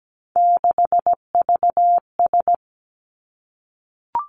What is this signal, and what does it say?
Morse code, heard as a keyed tone, about 17 wpm, 700 Hz. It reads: 6VS
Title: Three Challenging Letter-Number Combinations 17wpm